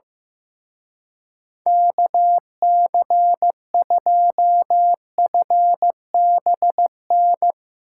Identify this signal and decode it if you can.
Morse code, heard as a keyed tone, about 16 wpm, 700 Hz. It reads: KC2FBN
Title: Speed Racing - USA 2x3 Call Signs 15wpm